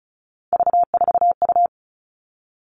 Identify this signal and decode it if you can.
Morse code, heard as a keyed tone, about 34 wpm, 700 Hz. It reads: V4U
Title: All Cut Numbers - 3-Digit 35wpm